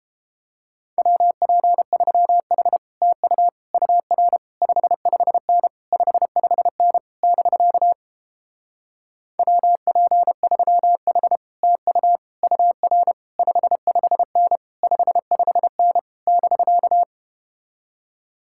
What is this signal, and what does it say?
Morse code, heard as a keyed tone, about 31 wpm, 700 Hz. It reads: WP3H TU UR 55N 55N <BK> WP3H TU UR 55N 55N <BK>